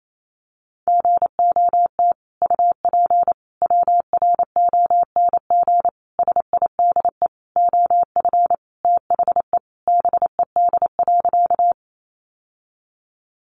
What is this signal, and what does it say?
Morse code, heard as a keyed tone, about 28 wpm, 700 Hz. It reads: GOT UP WRONG SIDE OF THE BED.
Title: Sentences from Top 900 Words 28wpm